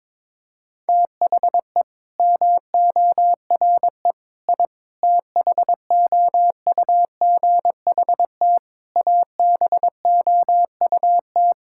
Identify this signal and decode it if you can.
Morse code, heard as a keyed tone, about 22 wpm, 700 Hz. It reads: THE MORE I THOUGHT ABOUT
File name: practice / 22wpm / Sets of 8 Words - 22wpm